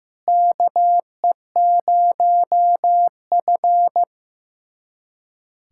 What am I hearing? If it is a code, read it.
Morse code, heard as a keyed tone, about 15 wpm, 700 Hz. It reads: KE0F
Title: USA Callsign DE Callsign - 4 Characters - 15wpm